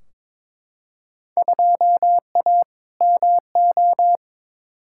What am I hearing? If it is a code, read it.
Morse code, heard as a keyed tone, about 22 wpm, 700 Hz. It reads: 2A MO
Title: Field Day Exchanges Encore - 22wpm